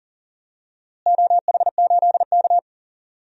Morse code, heard as a keyed tone, about 37 wpm, 700 Hz. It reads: OH8K
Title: DXCC-100 Call Signs - 4 Characters - Repeated 2x - 40wpm